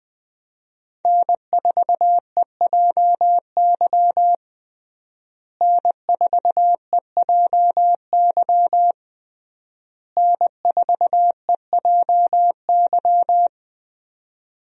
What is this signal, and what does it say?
Morse code, heard as a keyed tone, about 20 wpm, 700 Hz. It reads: N4EJY N4EJY N4EJY
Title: USA Call Signs - 5 Characters - Repeated 3x - 20wpm